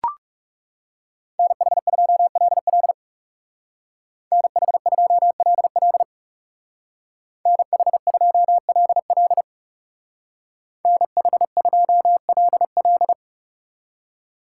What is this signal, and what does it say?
Morse code, heard as a keyed tone, about 40 wpm, 700 Hz. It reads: NH2LL NH2LL NH2LL NH2LL